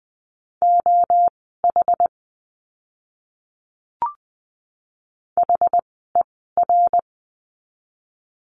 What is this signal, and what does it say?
Morse code, heard as a keyed tone, about 20 wpm, spaced as for 14 wpm, 700 Hz. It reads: OH HER
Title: 075 Lesson 13f - Letter L - Words 20wpm